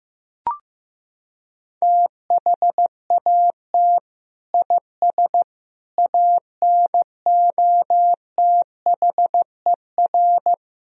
Morse code, heard as a keyed tone, about 15 wpm, 700 Hz. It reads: THAT IS ANOTHER
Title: Sentences from Top 1000 Words - Review Part 2 - 15wpm